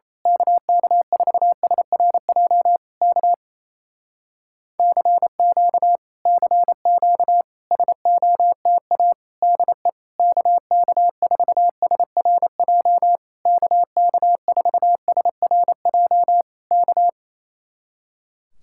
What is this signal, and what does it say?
Morse code, heard as a keyed tone, about 26 wpm, 700 Hz. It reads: KK4SRJ K CQ CQ SOTA DE KK4SRJ KK4SRJ K
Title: Speed-Racing Common QSO Phrases Encore - 28wpm